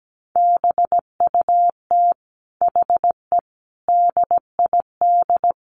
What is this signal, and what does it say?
Morse code, heard as a keyed tone, about 17 wpm, 700 Hz. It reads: BUT HE DID
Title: Sentences from Top 1400 Words 17wpm